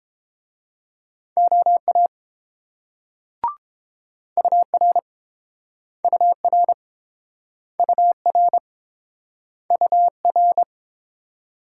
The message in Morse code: OA UR UR UR UR

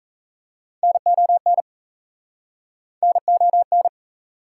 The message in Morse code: NON NON

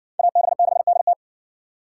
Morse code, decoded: AB6DT

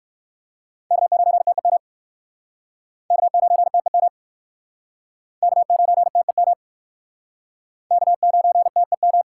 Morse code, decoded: K9TEM K9TEM K9TEM K9TEM

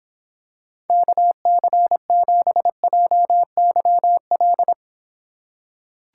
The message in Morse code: KC7JYL

